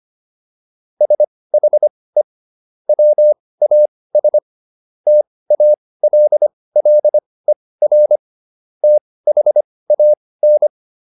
Morse code, heard as a keyed tone, about 25 wpm, spaced as for 18 wpm, 600 Hz. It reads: SHE WAS TALLER THAN
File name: SS123 18